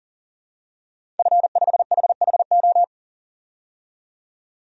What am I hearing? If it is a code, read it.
Morse code, heard as a keyed tone, about 37 wpm, 700 Hz. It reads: F5HHO
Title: DXCC-100 Call Signs - 5 Characters - 40wpm